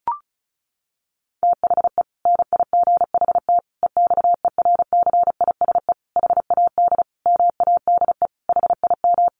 Morse code, heard as a keyed tone, about 34 wpm, 700 Hz. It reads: THE NIGHT EXERCISE HAD MADE HIM
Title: Sentences from Top 1400 Words 35wpm